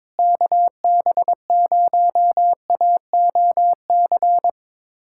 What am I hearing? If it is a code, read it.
Morse code, heard as a keyed tone, about 22 wpm, 700 Hz. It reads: KB0AOC
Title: USA Callsign DE Callsign - 6 Characters - 22wpm